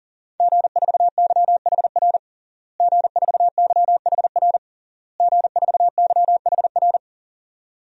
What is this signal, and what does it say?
Morse code, heard as a keyed tone, about 40 wpm, 700 Hz. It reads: G4YHR G4YHR G4YHR
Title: DXCC-100 Call Signs - 5 Characters - Repeated 3x - 40wpm